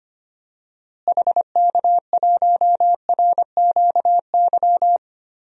Morse code, heard as a keyed tone, about 25 wpm, 700 Hz. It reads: HK1RQY
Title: DXCC-100 Call Signs - 6 Characters - Repeated 2x - 25wpm